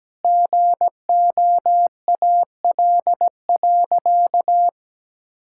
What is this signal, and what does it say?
Morse code, heard as a keyed tone, about 17 wpm, 700 Hz. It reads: GOAL.
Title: Sentences from Top 1000 Words - Review Part 5 - 17wpm